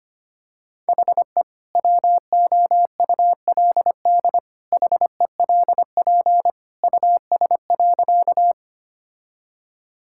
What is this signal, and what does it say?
Morse code, heard as a keyed tone, about 25 wpm, 700 Hz. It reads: HE WOULD HELP US.
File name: practice / 25wpm / Sentences with 2-Word Phrases - 25wpm